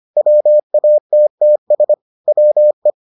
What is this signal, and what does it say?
Morse code, heard as a keyed tone, about 25 wpm, 600 Hz. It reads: WATTS WE